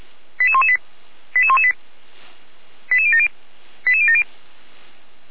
Five-tone selective call (ZVEI), heard as quick runs of five tones, tones of about 70 ms. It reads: 88118 88118 88070 88070